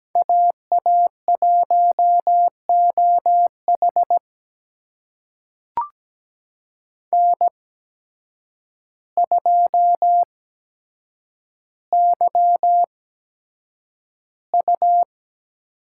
Morse code, heard as a keyed tone, about 17 wpm, spaced as for 8 wpm, 700 Hz. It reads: AA1OH N2YU
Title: USA Call Signs - 5 Characters - 8x Character Spacing - 17wpm